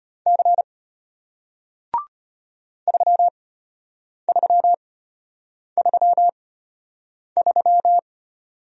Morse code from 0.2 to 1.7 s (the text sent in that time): C